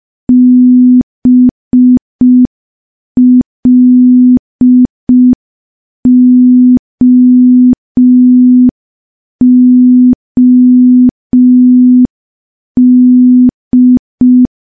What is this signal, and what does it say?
Morse code, heard as a keyed tone, about 5 wpm, 250 Hz. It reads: BLOOD
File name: morse / morse-training-05WPM